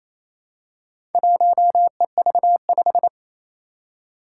Morse code, heard as a keyed tone, about 28 wpm, 700 Hz. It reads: 1EV5
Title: Four Challenging Letter-Number Combinations 28wpm